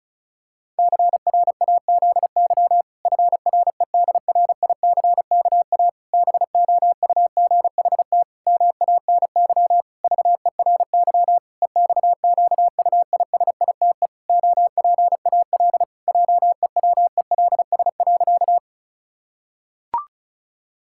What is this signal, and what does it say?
Morse code, heard as a keyed tone, about 36 wpm, 700 Hz. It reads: CRAZY FREDRICKA BOUGHT MANY VERY EXQUISITE OPAL JEWELS.
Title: Pangram Challenge v2 - 35wpm